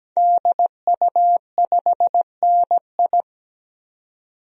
Morse code, heard as a keyed tone, about 17 wpm, 700 Hz. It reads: DU5NI
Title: DXCC-100 Call Signs - 5 Characters - 17wpm